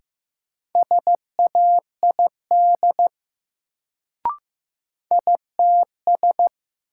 Morse code, heard as a keyed tone, about 15 wpm, 700 Hz. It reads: SAID ITS
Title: Sets of 2 Words - New Word Drill - 15wpm